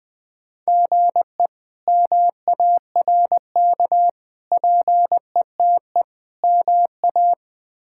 Morse code, heard as a keyed tone, about 20 wpm, 700 Hz. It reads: GE MARK PETE MA
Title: K1USN SST QSO Practice - 20wpm